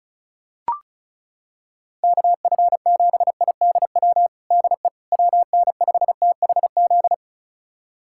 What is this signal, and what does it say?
Morse code, heard as a keyed tone, about 34 wpm, 690 Hz. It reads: KF7IDW DE WN5THZ
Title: USA Callsign DE Callsign - 6 Characters - 35wpm